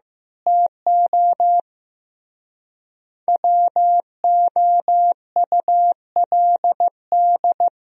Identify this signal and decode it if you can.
Morse code, heard as a keyed tone, about 15 wpm, 700 Hz. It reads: TO WOULD
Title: Speed-Racing - Sets of 3 Words 15wpm